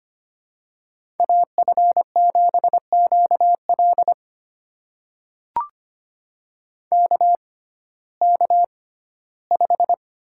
Morse code, heard as a keyed tone, about 25 wpm, spaced as for 15 wpm, 700 Hz. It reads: AF7QL KK5
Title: USA Call Signs - 5 Characters - 6x Character Spacing - 25wpm